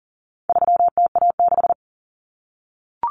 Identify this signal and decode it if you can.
Morse code, heard as a keyed tone, about 40 wpm, 700 Hz. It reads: 3TA6
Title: Common Cut Numbers - 4-Digit 40wpm